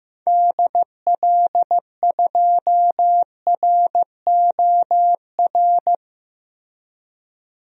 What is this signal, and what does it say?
Morse code, heard as a keyed tone, about 15 wpm, 700 Hz. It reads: DL2ROR
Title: DXCC-100 Call Signs - 6 Characters - Repeated 2x - 15wpm